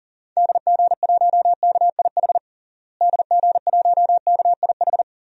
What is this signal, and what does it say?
Morse code, heard as a keyed tone, about 40 wpm, 700 Hz. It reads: DG1KIH DG1KIH
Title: DXCC-100 Call Signs - 6 Characters - Repeated 3x - 40wpm